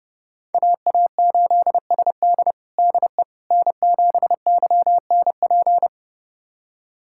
Morse code, heard as a keyed tone, about 31 wpm, 700 Hz. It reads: AA8SD DE N7YNP
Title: USA Callsign DE Callsign - 5 Characters - 30wpm